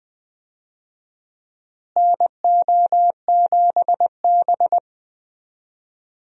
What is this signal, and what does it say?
Morse code, heard as a keyed tone, about 21 wpm, 700 Hz. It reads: NO7B
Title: USA Call Signs - 4 Characters - Repeated 2x - 20wpm